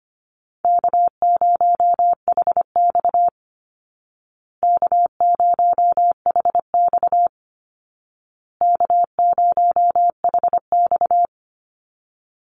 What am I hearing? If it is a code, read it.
Morse code, heard as a keyed tone, about 25 wpm, 700 Hz. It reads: K0HX K0HX K0HX
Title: Call Signs Repeated 2x - 25wpm